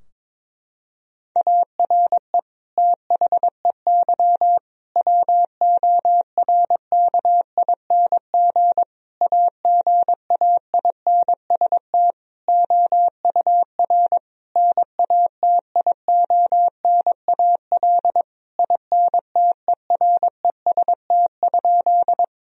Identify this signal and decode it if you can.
Morse code, heard as a keyed tone, about 22 wpm, 700 Hz. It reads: ARE THEY WORKING AGAINST OUR NATIONAL INTEREST?
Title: Sentences from Top 1000 Words - Review Part 4 - 22wpm